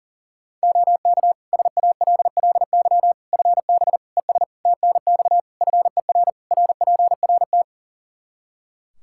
Morse code, heard as a keyed tone, about 40 wpm, 700 Hz. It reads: OK SALLY FB ES TNX FER RPRT
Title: Practice QSO - Exchange 2 - Info - 40wpm